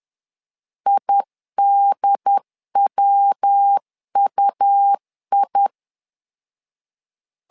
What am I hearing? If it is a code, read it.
Morse code, heard as a keyed tone, about 10 wpm, 800 Hz. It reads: IDWUI
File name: LEZIONI / 3 MWRO / 3 MWR 10wpm 1952